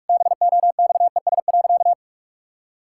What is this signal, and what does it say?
Morse code, heard as a keyed tone, about 46 wpm, 700 Hz. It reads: BOXES.
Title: Pangram Challenge v2 - 45wpm